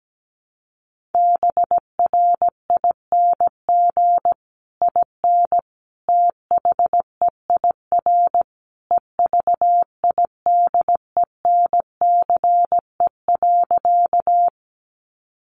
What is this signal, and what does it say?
Morse code, heard as a keyed tone, about 17 wpm, 700 Hz. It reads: BRING IN THEIR EVIDENCE.
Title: Sentences from Top 1300 Words 17wpm